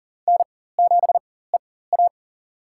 Morse code, heard as a keyed tone, about 39 wpm, spaced as for 18 wpm, 700 Hz. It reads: N7EA
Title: USA Call Signs - 4 Characters - 4x Character Spacing - 40wpm